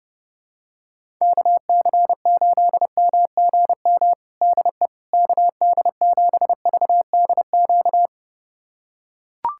Morse code, heard as a keyed tone, about 30 wpm, 700 Hz. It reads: KC8MGM DE KD7VDQ
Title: USA Callsign DE Callsign - 6 Characters - 30wpm